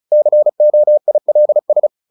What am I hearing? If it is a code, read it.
Morse code, heard as a keyed tone, about 34 wpm, 600 Hz. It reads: COILS
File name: the old man and the sea / the old man and the sea 1 35wpm